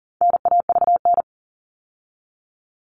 Morse code, heard as a keyed tone, about 40 wpm, 700 Hz. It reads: NAVN
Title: All Cut Numbers - 4-Digit 40wpm